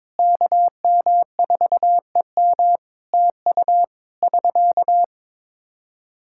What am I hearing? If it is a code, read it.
Morse code, heard as a keyed tone, about 22 wpm, 700 Hz. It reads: KM4EM TU <SK>
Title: Practice QSO - Exchange 4 - Ending - 22wpm